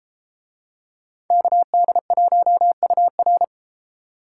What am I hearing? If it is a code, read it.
Morse code, heard as a keyed tone, about 32 wpm, 700 Hz. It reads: KD1UR